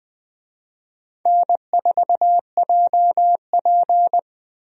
Morse code, heard as a keyed tone, about 20 wpm, 700 Hz. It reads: N4JP
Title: USA Call Signs - 4 Characters - 20wpm